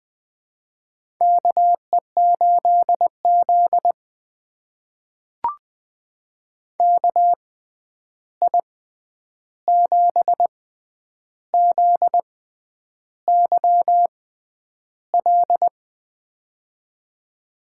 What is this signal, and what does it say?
Morse code, heard as a keyed tone, about 20 wpm, spaced as for 9 wpm, 700 Hz. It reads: KE8Z KI7ZYL
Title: USA Call Signs - 6x Character Spacing - 20wpm